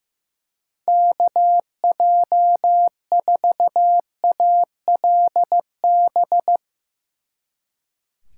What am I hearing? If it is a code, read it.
Morse code, heard as a keyed tone, about 17 wpm, 700 Hz. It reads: KJ4ALB
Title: Speed Racing - USA 2x3 Call Signs 15wpm